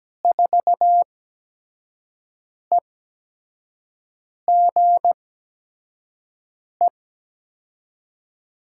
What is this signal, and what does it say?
Morse code, heard as a keyed tone, about 17 wpm, spaced as for 5 wpm, 700 Hz. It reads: 4EGE
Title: USA Call Signs - 8x Character Spacing - 17wpm